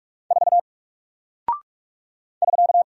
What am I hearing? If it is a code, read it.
Morse code, heard as a keyed tone, about 44 wpm, 700 Hz. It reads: 4 <SK>
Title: Single Character (Letter:Number:Punctuation:Prosign) 45wpm